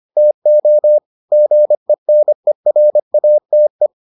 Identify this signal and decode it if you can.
Morse code, heard as a keyed tone, about 25 wpm, 600 Hz. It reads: TO GENERATE